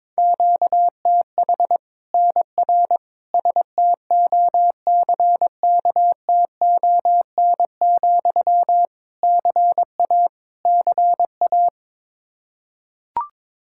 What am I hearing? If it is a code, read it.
Morse code, heard as a keyed tone, about 22 wpm, 700 Hz. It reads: QTH NR STOCKTON, CA CA
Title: Practice QSO - Exchange 1 - Beginning - 22wpm